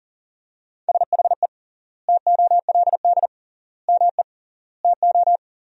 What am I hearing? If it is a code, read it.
Morse code, heard as a keyed tone, about 38 wpm, 700 Hz. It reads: SHE TOLD ME TO
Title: 3x Word-Spacing - Sentences from Top 400 Words - 40wpm